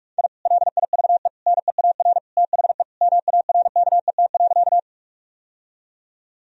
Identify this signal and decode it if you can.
Morse code, heard as a keyed tone, about 41 wpm, 700 Hz. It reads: I LIVE NEAR THE MARKET.
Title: Sentences from Top 1000 Words - Review Part 3 - 45wpm